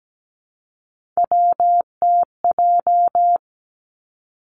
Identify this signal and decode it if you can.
Morse code, heard as a keyed tone, about 17 wpm, 700 Hz. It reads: WTJ
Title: Three Challenging Letter-Number Combinations 17wpm